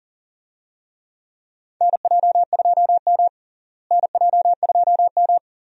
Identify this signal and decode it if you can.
Morse code, heard as a keyed tone, about 40 wpm, 700 Hz. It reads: NJ2M NJ2M
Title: USA Call Signs - 4 Characters - Repeated 2x - 40wpm